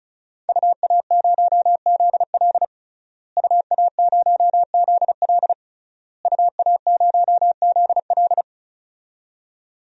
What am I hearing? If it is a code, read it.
Morse code, heard as a keyed tone, about 36 wpm, 700 Hz. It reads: UA0ZL UA0ZL UA0ZL
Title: DXCC-100 Call Signs - 5 Characters - Repeated 3x - 35wpm